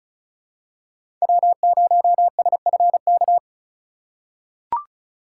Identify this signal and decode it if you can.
Morse code, heard as a keyed tone, about 36 wpm, 700 Hz. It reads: W0SFK
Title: USA Call Signs - 5 Characters - 3x Character Spacing - 35wpm